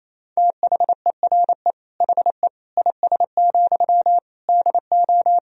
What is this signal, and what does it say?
Morse code, heard as a keyed tone, about 28 wpm, 700 Hz. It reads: THERE HE IS, DO
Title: Sentences from Top 100 Words and Abbreviations 28wpm